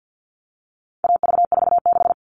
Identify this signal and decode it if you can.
Morse code, heard as a keyed tone, about 50 wpm, 700 Hz. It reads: AV46
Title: All Cut Numbers - 4-Digit 50wpm